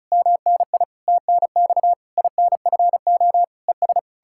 Morse code, heard as a keyed tone, about 46 wpm, 700 Hz. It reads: MNI TNX INFO ES